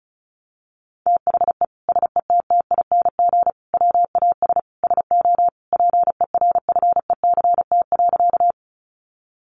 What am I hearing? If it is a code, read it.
Morse code, heard as a keyed tone, about 34 wpm, 700 Hz. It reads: THE SETTING WAS SO PERFECT.
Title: Sentences from Top 1300 Words 35wpm